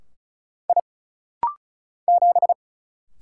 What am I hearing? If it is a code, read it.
Morse code, heard as a keyed tone, about 35 wpm, 700 Hz. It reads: I 7
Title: Single Letter-Number - Rapid-Fire 35wpm